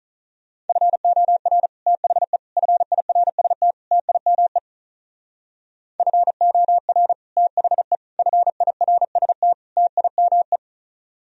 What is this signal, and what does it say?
Morse code, heard as a keyed tone, about 43 wpm, 700 Hz. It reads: FOR THE FIRST TIME FOR THE FIRST TIME
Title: Speed-Racing - Sets of 4 Words 35wpm